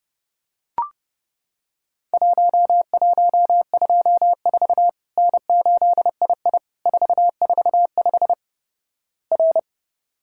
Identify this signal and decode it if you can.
Morse code, heard as a keyed tone, about 30 wpm, 700 Hz. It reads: 1124 N8II 445 R
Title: Worked All Europe DX Contest - QTC Exchange - 30wpm